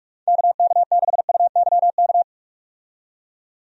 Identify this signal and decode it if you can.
Morse code, heard as a keyed tone, about 46 wpm, 700 Hz. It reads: KK6UYK
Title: USA Call Signs - 6 Characters - 45wpm